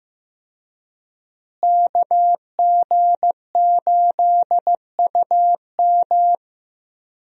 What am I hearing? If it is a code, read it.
Morse code, heard as a keyed tone, about 15 wpm, 700 Hz. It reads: KG8UM
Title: USA Call Signs - 5 Characters - Repeated 2x - 15wpm